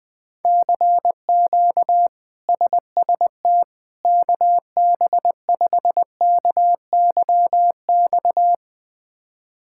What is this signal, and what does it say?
Morse code, heard as a keyed tone, about 20 wpm, 700 Hz. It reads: CQ SST KB5KYX
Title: K1USN SST QSO Practice - 20wpm